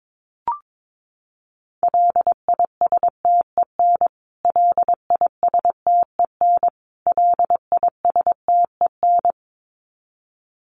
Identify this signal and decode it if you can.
Morse code, heard as a keyed tone, about 22 wpm, 700 Hz. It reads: LISTEN LISTEN LISTEN
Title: Top 500 Words Repeated 22wpm